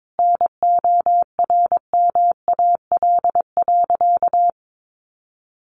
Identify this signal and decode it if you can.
Morse code, heard as a keyed tone, about 22 wpm, 700 Hz. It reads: NORMAL.
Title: Sentences from Top 1200 Words 22wpm